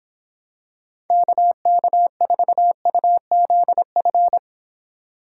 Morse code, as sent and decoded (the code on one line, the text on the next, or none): -.- -.- ....- ..- --.. ..-.
KK4UZF